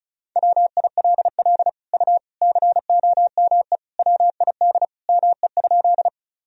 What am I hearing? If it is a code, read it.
Morse code, heard as a keyed tone, about 36 wpm, 700 Hz. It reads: WILL U COME WID ME?
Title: Sentences from Top 200 Words and Abbreviations 35wpm